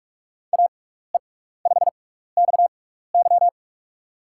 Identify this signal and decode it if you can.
Morse code, heard as a keyed tone, about 44 wpm, spaced as for 15 wpm, 700 Hz. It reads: AE5XY
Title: USA Call Signs - 6x Character Spacing - 45wpm